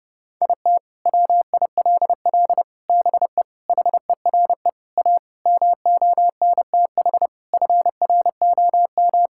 Morse code, heard as a keyed tone, about 30 wpm, 700 Hz. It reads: IT WILL BE HERE A MONTH FROM
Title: Sentences from Top 1000 Words - Review Part 5 - 30wpm